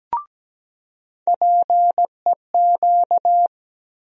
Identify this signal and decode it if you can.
Morse code, heard as a keyed tone, about 17 wpm, 700 Hz. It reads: PEQ
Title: USA Call Sign Postfixes - 3 Letters 17wpm